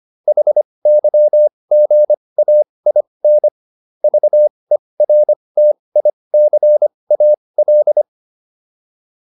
Morse code, heard as a keyed tone, about 25 wpm, spaced as for 20 wpm, 600 Hz. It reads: HYGAIN VERTICAL